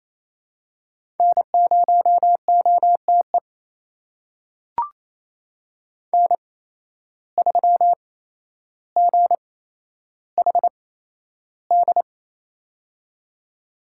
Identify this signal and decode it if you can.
Morse code, heard as a keyed tone, about 28 wpm, spaced as for 12 wpm, 700 Hz. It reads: N0OTE N3GHD
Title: USA Call Signs - 5 Characters - 8x Character Spacing - 28wpm